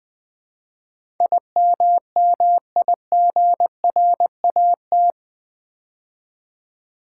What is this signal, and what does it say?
Morse code, heard as a keyed tone, about 20 wpm, 700 Hz. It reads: IMMIGRAT
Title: Build A Word - 4 Letters At A Time - 11-Letter Words - 20wpm